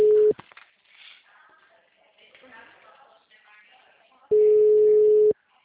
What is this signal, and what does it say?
Ringback tone (Europe, 425 Hz), with speech.
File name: out- -2010- - - .41591